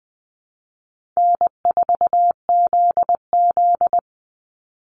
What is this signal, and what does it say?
Morse code, heard as a keyed tone, about 20 wpm, 700 Hz. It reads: N4ZZ
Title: Call Signs Repeated 3x - 20wpm